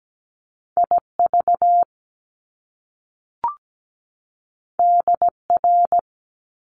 Morse code, heard as a keyed tone, about 17 wpm, 700 Hz. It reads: IV DR
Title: Top Two Letter Combinations in Words 17wpm